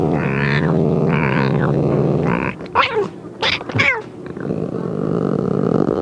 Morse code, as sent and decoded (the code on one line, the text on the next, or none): none